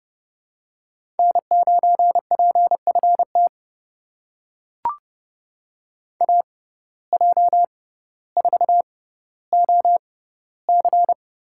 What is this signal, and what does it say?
Morse code, heard as a keyed tone, about 30 wpm, spaced as for 15 wpm, 700 Hz. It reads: N9PFT AJ4OC
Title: USA Call Signs - 5 Characters - 6x Character Spacing - 30wpm